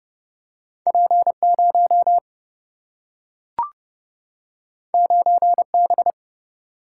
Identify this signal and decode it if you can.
Morse code, heard as a keyed tone, about 30 wpm, 700 Hz. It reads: P 0 9 B
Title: Single-Letter Single-Number 30wpm